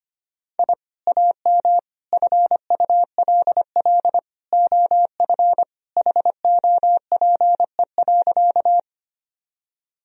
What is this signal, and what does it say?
Morse code, heard as a keyed tone, about 25 wpm, 700 Hz. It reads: I AM FULL OF HOPE.
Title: Sentences from Top 1000 Words - Review Part 3 - 25wpm